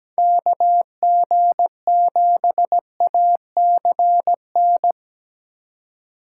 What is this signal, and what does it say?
Morse code, heard as a keyed tone, about 17 wpm, 700 Hz. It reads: KG7ACN
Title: USA Call Signs - 6 Characters - 4x Character Spacing - 17wpm